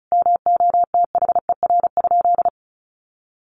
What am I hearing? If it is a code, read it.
Morse code, heard as a keyed tone, about 34 wpm, 700 Hz. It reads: MOTHER?
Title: Sentences from Top 200 Words 35wpm